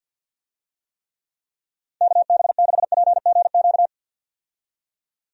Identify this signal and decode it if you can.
Morse code, heard as a keyed tone, about 47 wpm, 700 Hz. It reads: KB6PGQ
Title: USA Call Signs - 6 Characters - 50wpm